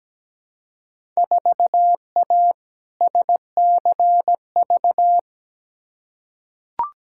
Morse code, heard as a keyed tone, about 17 wpm, 700 Hz. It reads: 4A SCV
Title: Field Day Exchanges Encore - 17wpm